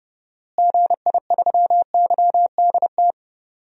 Morse code, heard as a keyed tone, about 30 wpm, 700 Hz. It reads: GI3YDT
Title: DXCC-100 Call Signs - 6 Characters - Repeated 2x - 30wpm